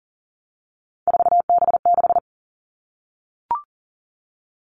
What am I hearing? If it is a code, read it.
Morse code, heard as a keyed tone, about 39 wpm, 700 Hz. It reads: 4B6
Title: All Cut Numbers - 3-Digit 40wpm